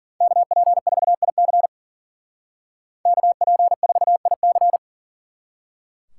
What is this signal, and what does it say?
Morse code, heard as a keyed tone, about 50 wpm, 700 Hz. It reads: KP4IC KP4IC
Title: Speed Racing - USA 2x2 Call Signs 40wpm